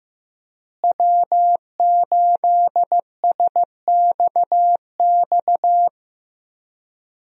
Morse code, heard as a keyed tone, about 16 wpm, 700 Hz. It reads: W8SXX
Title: USA Call Signs - 5 Characters - Repeated 3x - 15wpm